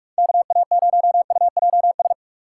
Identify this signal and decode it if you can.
Morse code, heard as a keyed tone, about 46 wpm, 700 Hz. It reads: KA0UJS
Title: USA Call Signs - 6 Characters - 2x Character Spacing - 45wpm